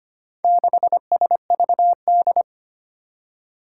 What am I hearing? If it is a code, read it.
Morse code, heard as a keyed tone, about 25 wpm, 700 Hz. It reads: 6SVD
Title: Four Challenging Letter-Number Combinations 25wpm